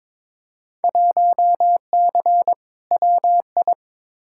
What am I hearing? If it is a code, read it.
Morse code, heard as a keyed tone, about 22 wpm, 700 Hz. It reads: 1C WI
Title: Field Day Exchanges - 22wpm